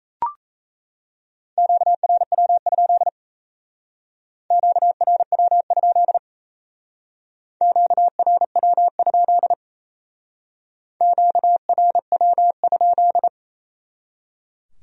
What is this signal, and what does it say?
Morse code, heard as a keyed tone, about 35 wpm, 700 Hz. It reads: QRW? QRW? QRW? QRW?
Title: Speed-Racing Q-Codes - 28wpm